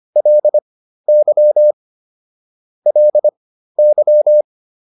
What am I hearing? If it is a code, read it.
Morse code, heard as a keyed tone, about 25 wpm, spaced as for 13 wpm, 600 Hz. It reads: LY LY
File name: LY5 13